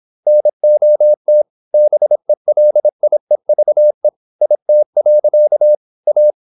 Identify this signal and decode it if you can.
Morse code, heard as a keyed tone, about 26 wpm, 600 Hz. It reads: NOT BELIEVE IT. A
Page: Never 26wpm | CW med Gnister